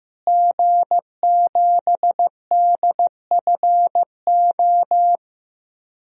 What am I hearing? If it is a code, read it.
Morse code, heard as a keyed tone, about 15 wpm, 700 Hz. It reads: G7DFO
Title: DXCC-100 Call Signs - 5 Characters - Repeated 2x - 15wpm